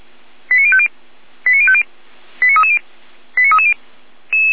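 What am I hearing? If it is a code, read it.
Five-tone selective call (ZVEI), heard as quick runs of five tones, tones of about 70 ms. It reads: 89050 89050 89330 89330
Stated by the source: ZVEI five-tone selective call received on SDR, from audio.f85360.feuerwehr-orlamuende.org (Feuerwehr Orlamünde fire service).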